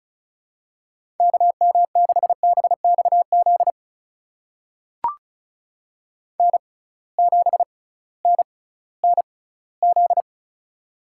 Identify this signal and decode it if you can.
Morse code, heard as a keyed tone, about 36 wpm, spaced as for 19 wpm, 700 Hz. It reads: KM6BXZ N7NNZ
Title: USA Call Signs - 6x Character Spacing - 35wpm